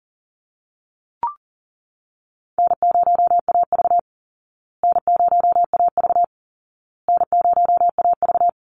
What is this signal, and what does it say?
Morse code, heard as a keyed tone, about 39 wpm, 700 Hz. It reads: N0AV N0AV N0AV
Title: Easy Call Signs Repeated 3x - 40wpm